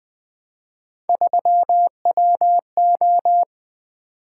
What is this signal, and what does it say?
Morse code, heard as a keyed tone, about 20 wpm, 700 Hz. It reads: 3WO
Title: Three Letter-Number Combinations - 20wpm